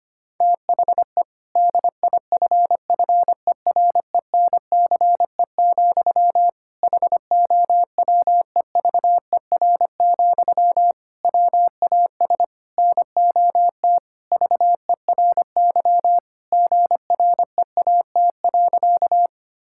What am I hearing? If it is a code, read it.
Morse code, heard as a keyed tone, about 25 wpm, 700 Hz. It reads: THE DIFFERENCE, HOWEVER, WAS NOT VERY GREAT.
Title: Sentences from Top 1000 Words - Review Part 5 - 25wpm